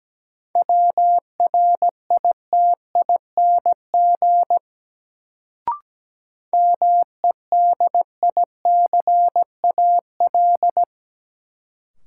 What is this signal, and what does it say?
Morse code, heard as a keyed tone, about 17 wpm, 700 Hz. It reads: WRITING MEDICAL
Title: Sentences from Top 800 Words - New Word Drill - 17wpm